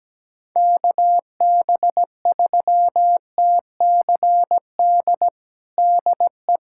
Morse code, heard as a keyed tone, about 17 wpm, 700 Hz. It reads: KB3TCD DE
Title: USA Callsign DE Callsign - 6 Characters - 17wpm